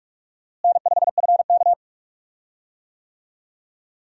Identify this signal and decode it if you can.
Morse code, heard as a keyed tone, about 43 wpm, 700 Hz. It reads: N5FK
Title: USA Call Signs - 4 Characters - Repeated 2x - 45wpm